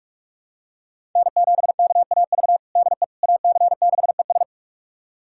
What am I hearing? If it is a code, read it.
Morse code, heard as a keyed tone, about 44 wpm, 700 Hz. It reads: N7KAV DE AC6ES
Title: USA Callsign DE Callsign - 5 Characters - 45wpm